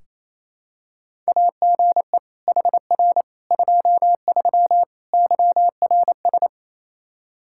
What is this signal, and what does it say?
Morse code, heard as a keyed tone, about 28 wpm, 700 Hz. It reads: AGE HR 23 YRS
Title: Practice QSO - Exchange 3 - More Info - 28wpm